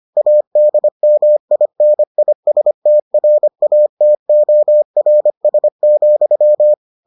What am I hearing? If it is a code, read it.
Morse code, heard as a keyed tone, about 25 wpm, 600 Hz. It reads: ADMINISTRATORS,